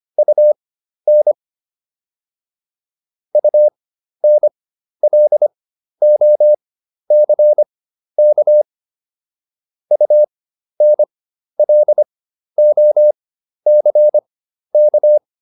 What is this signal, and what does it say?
Morse code, heard as a keyed tone, about 25 wpm, spaced as for 12 wpm, 600 Hz. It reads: UN UNLOCK UNLOCK
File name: UN6 12